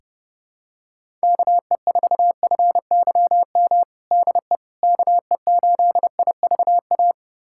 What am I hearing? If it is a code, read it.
Morse code, heard as a keyed tone, about 30 wpm, 700 Hz. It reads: KE4FYM DE KE8IVA
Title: USA Callsign DE Callsign - 6 Characters - 30wpm